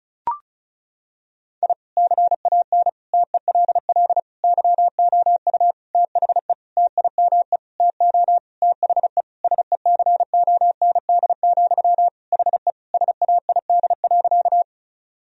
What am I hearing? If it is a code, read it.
Morse code, heard as a keyed tone, about 34 wpm, 690 Hz. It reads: I CAN TELL YOU THE TIME TO THE SECOND, HE SAID.
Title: Sentences from Top 1000 Words - Review Part 1 - 35wpm